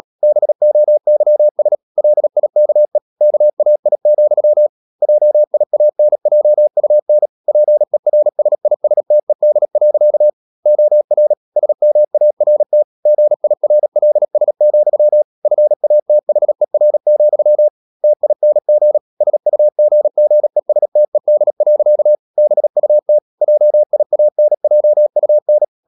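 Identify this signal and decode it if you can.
Morse code, heard as a keyed tone, about 37 wpm, 600 Hz. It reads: BOYS LIKE KAI, JIANJUN PERSISTED. OR SMART GIRLS, FATHER, TING SUGGESTED. BUT JIANJUN
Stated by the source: Morse code practice at 35–39 wpm sending English text of Ken Follett's novel Never, from CW med Gnister.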